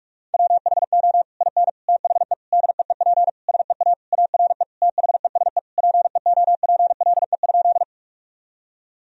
Morse code, heard as a keyed tone, about 46 wpm, 700 Hz. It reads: WHO IN THE DEEP SEA ARE THESE PEOPLE?
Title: Sentences from Top 1000 Words - Review Part 3 - 45wpm